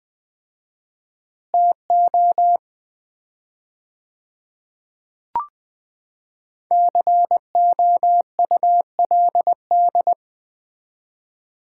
Morse code, heard as a keyed tone, about 20 wpm, 700 Hz. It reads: TO COULD
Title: 4x Word-Spacing - Sets of 2 Words - 20wpm